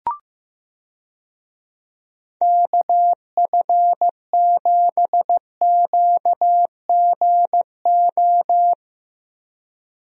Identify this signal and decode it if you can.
Morse code, heard as a keyed tone, about 15 wpm, 700 Hz. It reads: KF7QGO
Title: USA Call Signs - 6 Characters - Repeated 2x - 15wpm